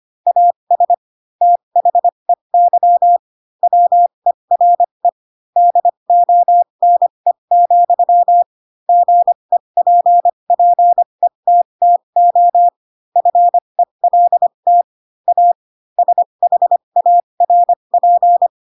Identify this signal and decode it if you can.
Morse code, heard as a keyed tone, about 25 wpm, spaced as for 22 wpm, 700 Hz. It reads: AS THEY WERE DONE, GEPPETTO FELT A SHARP
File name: Pinocchio / Pinocchio ch3 18 25 ik1hge